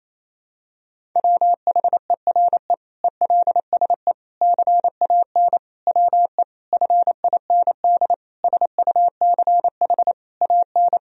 Morse code, heard as a keyed tone, about 28 wpm, 700 Hz. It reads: WHERE ELSE CAN WE FIND SUCH AN
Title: Sentences from Top 1000 Words - Review Part 1 - 28wpm